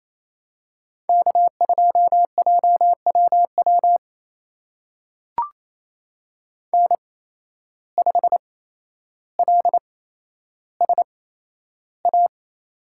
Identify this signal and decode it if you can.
Morse code, heard as a keyed tone, about 28 wpm, spaced as for 12 wpm, 700 Hz. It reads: K2JWW N5LSA
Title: USA Call Signs - 5 Characters - 8x Character Spacing - 28wpm